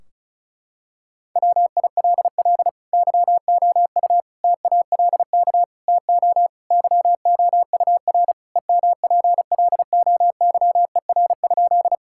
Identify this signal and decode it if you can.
Morse code, heard as a keyed tone, about 34 wpm, 700 Hz. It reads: WILL YOU TALK TO YOUR EMPLOYER?
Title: Sentences from Top 1600 Words 35wpm